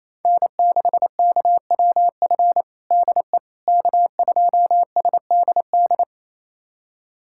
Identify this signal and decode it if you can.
Morse code, heard as a keyed tone, about 27 wpm, 700 Hz. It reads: N6KWF DE K2SDD
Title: USA Callsign DE Callsign - 5 Characters - 28wpm